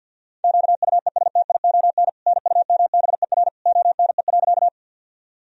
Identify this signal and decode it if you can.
Morse code, heard as a keyed tone, about 47 wpm, 700 Hz. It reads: QUESTION NUMBER ONE.
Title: Sentences from Top 1000 Words - Review Part 1 - 50wpm